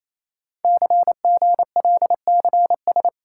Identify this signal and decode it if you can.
Morse code, heard as a keyed tone, about 27 wpm, 700 Hz. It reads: CGLCS
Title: Code Groups - 5 Letters - 28wpm